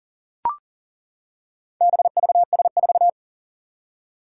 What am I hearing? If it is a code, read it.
Morse code, heard as a keyed tone, about 40 wpm, 700 Hz. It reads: BVS4
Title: Four Challenging Letter-Number Combinations 40wpm